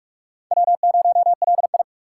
Morse code, heard as a keyed tone, about 46 wpm, 700 Hz. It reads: W0LI
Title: USA Call Signs - 4 Characters - 3x Character Spacing - 45wpm